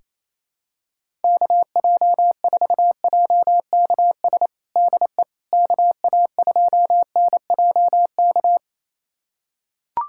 Morse code, heard as a keyed tone, about 28 wpm, 700 Hz. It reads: KJ4JKS DE KA2NJK
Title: USA Callsign DE Callsign - 6 Characters - 28wpm